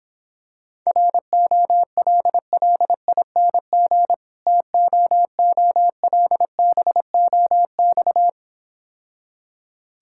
Morse code, heard as a keyed tone, about 26 wpm, 700 Hz. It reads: ROLLING TOOLBOX